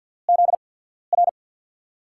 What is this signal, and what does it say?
Morse code, heard as a keyed tone, about 52 wpm, spaced as for 14 wpm, 700 Hz. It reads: ZR
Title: USA Call Signs - 5 Characters - 8x Character Spacing - 50wpm